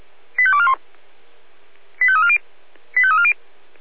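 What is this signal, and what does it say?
Five-tone selective call (ZVEI), heard as quick runs of five tones, tones of about 70 ms. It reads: 85241 85340 85340
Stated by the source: ZVEI five-tone call 85886 received with an SDR.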